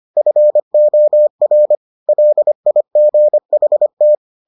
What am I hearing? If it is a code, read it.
Morse code, heard as a keyed tone, about 25 wpm, 600 Hz. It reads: FOR LIGHT